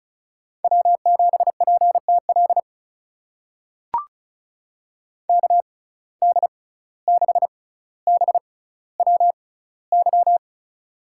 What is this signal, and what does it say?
Morse code, heard as a keyed tone, about 36 wpm, spaced as for 17 wpm, 700 Hz. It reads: W7PTL KD6BWY
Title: USA Call Signs - 6x Character Spacing - 35wpm